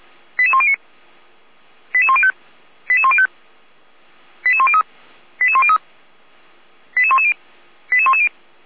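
Five-tone selective call (ZVEI), heard as quick runs of five tones, tones of about 70 ms. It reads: 88109 88105 88105 88103 88103 88110 88110
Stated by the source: ZVEI five-tone selective call received by SDR, from the Feuerwehr Orlamünde (fire brigade) audio archive.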